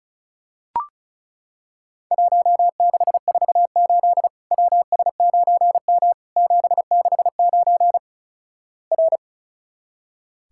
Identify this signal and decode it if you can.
Morse code, heard as a keyed tone, about 36 wpm, 700 Hz. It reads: 1648 WS9M 769 R
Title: Worked All Europe DX Contest - QTC Exchange - 35wpm